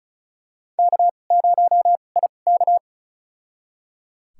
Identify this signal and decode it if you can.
Morse code, heard as a keyed tone, about 36 wpm, spaced as for 25 wpm, 690 Hz. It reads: K0IK
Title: USA Call Signs - 4 Characters - 2x Character Spacing - 35wpm